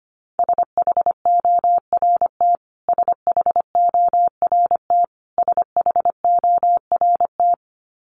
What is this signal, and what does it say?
Morse code, heard as a keyed tone, about 25 wpm, 700 Hz. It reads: SHORT SHORT SHORT
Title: Top 500 Words Repeated 25wpm